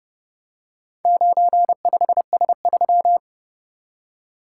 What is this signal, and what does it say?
Morse code, heard as a keyed tone, about 31 wpm, 700 Hz. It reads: 95S3